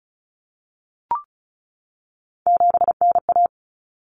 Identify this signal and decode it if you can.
Morse code, heard as a keyed tone, about 34 wpm, 700 Hz. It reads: 7NA
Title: Common Cut Numbers - 3-Digit 35wpm